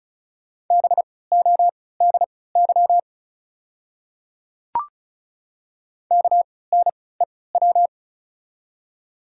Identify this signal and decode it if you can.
Morse code, heard as a keyed tone, about 34 wpm, spaced as for 20 wpm, 690 Hz. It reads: BODY KNEW
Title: 3x Character Spacing - Top 500 Words - 35wpm